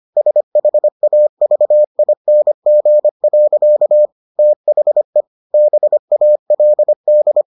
Morse code, heard as a keyed tone, about 25 wpm, 600 Hz. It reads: SHAVING. THE BALD